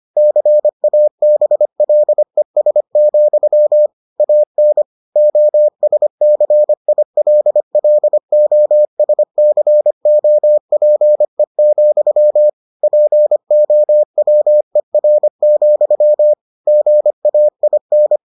Morse code, heard as a keyed tone, about 25 wpm, 600 Hz. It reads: CABLES, AN OSCILLOSCOPE, POWER, GAIN